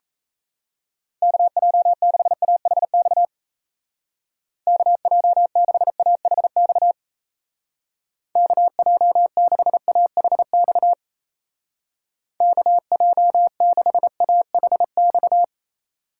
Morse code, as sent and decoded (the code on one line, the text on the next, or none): -.- .--- -.... .- .... -..- / -.- .--- -.... .- .... -..- / -.- .--- -.... .- .... -..- / -.- .--- -.... .- .... -..-
KJ6AHX KJ6AHX KJ6AHX KJ6AHX